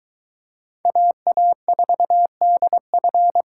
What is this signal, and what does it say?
Morse code, heard as a keyed tone, about 23 wpm, 700 Hz. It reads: AA4DF